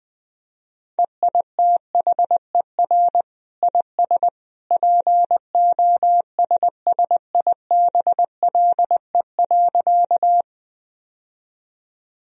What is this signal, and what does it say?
Morse code, heard as a keyed tone, about 20 wpm, 700 Hz. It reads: EITHER IS POSSIBLE.
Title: Sentences from Top 1000 Words - Review Part 4 - 20wpm